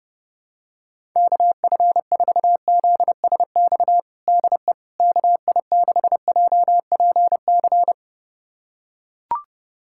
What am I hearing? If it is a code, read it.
Morse code, heard as a keyed tone, about 30 wpm, 700 Hz. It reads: KF4ZSX DE KI6JPC
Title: USA Callsign DE Callsign 30wpm